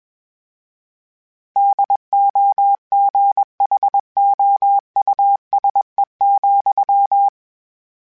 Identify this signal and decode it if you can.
Morse code, heard as a keyed tone, about 21 wpm, 800 Hz. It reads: DOGHOUSE,
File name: General / General-mp3s / Houses @20